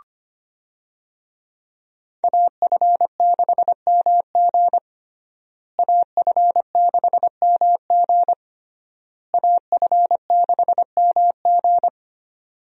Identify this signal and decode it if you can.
Morse code, heard as a keyed tone, about 25 wpm, 700 Hz. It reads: AF6MG AF6MG AF6MG
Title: USA Call Signs - 5 Characters - Repeated 3x - 25wpm